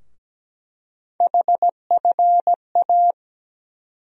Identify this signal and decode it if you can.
Morse code, heard as a keyed tone, about 17 wpm, 700 Hz. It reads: HFA
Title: USA Call Sign Postfixes - 3 Letters 17wpm